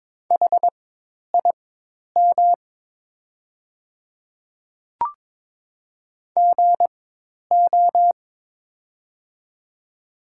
Morse code, heard as a keyed tone, about 22 wpm, spaced as for 10 wpm, 700 Hz. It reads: HIM GO
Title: 4x Character Spacing - Top 100 Words - 22wpm